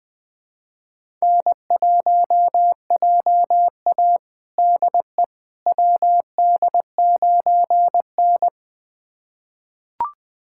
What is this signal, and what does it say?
Morse code, heard as a keyed tone, about 20 wpm, 700 Hz. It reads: N1JA DE WD9N
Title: USA Callsign DE Callsign - 4 Characters - 20wpm